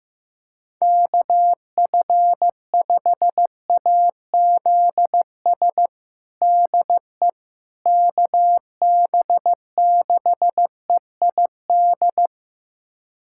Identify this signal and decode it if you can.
Morse code, heard as a keyed tone, about 15 wpm, 700 Hz. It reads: KF5AZS DE KB6EID
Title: USA Callsign DE Callsign - 6 Characters - 15wpm